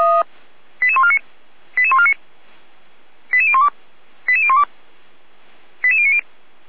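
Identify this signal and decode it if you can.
Five-tone selective call (ZVEI), heard as quick runs of five tones, tones of about 70 ms. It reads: 88149 88149 88012 88012 88090
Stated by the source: ZVEI five-tone selective call received by SDR, archived by Feuerwehr Orlamünde.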